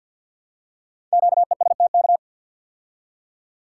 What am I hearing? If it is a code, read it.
Morse code, heard as a keyed tone, about 48 wpm, 700 Hz. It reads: QESTK
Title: Code Groups - 5 Letters - 50wpm